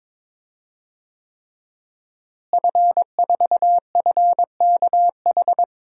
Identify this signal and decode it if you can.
Morse code, heard as a keyed tone, about 22 wpm, 700 Hz. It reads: F4FKH
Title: DXCC-100 Call Signs - 5 Characters - Repeated 3x - 22wpm